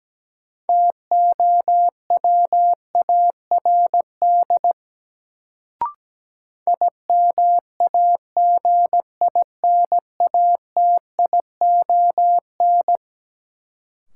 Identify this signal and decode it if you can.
Morse code, heard as a keyed tone, about 17 wpm, 700 Hz. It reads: TOWARD IMAGINATION
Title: Sentences from Top 1300 Words - New Word Drill Alt - 17wpm